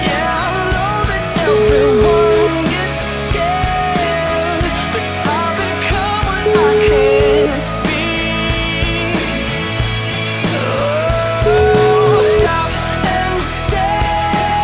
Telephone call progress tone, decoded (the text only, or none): ringback tone